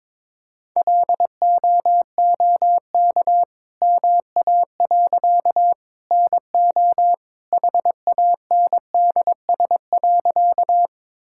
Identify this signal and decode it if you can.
Morse code, heard as a keyed tone, about 21 wpm, 690 Hz. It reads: LOOK MA. NO HANDS.
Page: morse-code-ninja-practice-22wpm